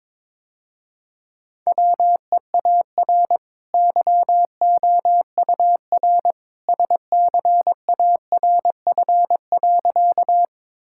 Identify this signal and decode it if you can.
Morse code, heard as a keyed tone, about 22 wpm, 700 Hz. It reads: WEAR YOUR SCARF.
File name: practice / 22wpm / Easy Sentences with Hints - 22wpm